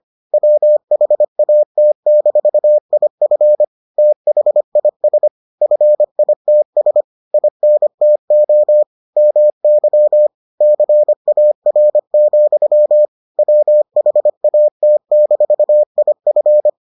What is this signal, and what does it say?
Morse code, heard as a keyed tone, about 25 wpm, 600 Hz. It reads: WHAT-IF THIS FITS INTO MY CAR, WHAT-IF